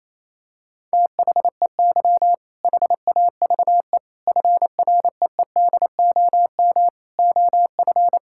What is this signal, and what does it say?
Morse code, heard as a keyed tone, about 27 wpm, 700 Hz. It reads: THEY HAVE FREEDOM OF
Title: Sentences from Top 1000 Words - Review Part 5 - 28wpm